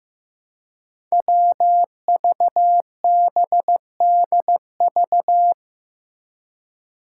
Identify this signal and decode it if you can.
Morse code, heard as a keyed tone, about 15 wpm, 700 Hz. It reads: WVBDV
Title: Five Challenging Letter-Number Combinations 15wpm